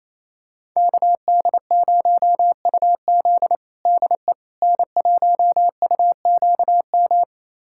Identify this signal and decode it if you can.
Morse code, heard as a keyed tone, about 28 wpm, 700 Hz. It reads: KD0UZ DE N1UQM
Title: USA Callsign DE Callsign - 5 Characters - 28wpm